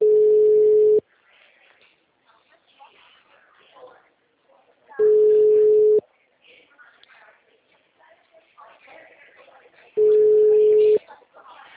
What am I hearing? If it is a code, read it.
Ringback tone (Europe, 425 Hz), with speech.
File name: out- -2005- - - .5487